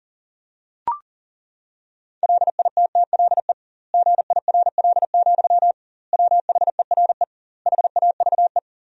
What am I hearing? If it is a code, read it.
Morse code, heard as a keyed tone, about 37 wpm, 700 Hz. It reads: LITTLE GIRL, WHERE HAVE
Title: 2x Word-Spacing - Sentences from Top 400 Words - 40wpm